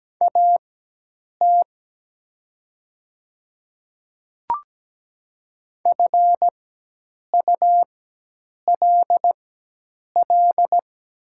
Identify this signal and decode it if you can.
Morse code, heard as a keyed tone, about 17 wpm, spaced as for 8 wpm, 700 Hz. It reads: AT FULL
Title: 4x Character Spacing - Top 500 Words - 17wpm